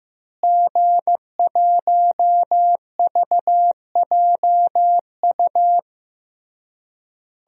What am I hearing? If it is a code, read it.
Morse code, heard as a keyed tone, about 15 wpm, 700 Hz. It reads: G1VJU
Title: DXCC-100 Call Signs - 5 Characters - Repeated 3x - 15wpm